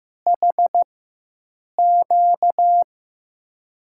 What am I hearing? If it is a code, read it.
Morse code, heard as a keyed tone, about 15 wpm, spaced as for 7 wpm, 700 Hz. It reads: HQ
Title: USA Call Signs - 5 Characters - 4x Character Spacing - 15wpm